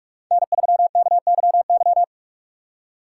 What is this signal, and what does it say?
Morse code, heard as a keyed tone, about 46 wpm, 700 Hz. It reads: N3KYY
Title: USA Call Signs - 5 Characters - 6x Character Spacing - 45wpm